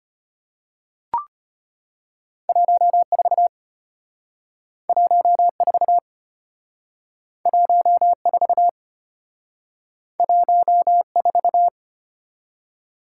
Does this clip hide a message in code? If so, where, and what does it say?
Yes: Morse code from 2.5 to 11.7 s, heard as a keyed tone, about 33 wpm, 700 Hz. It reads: 14 14 14 14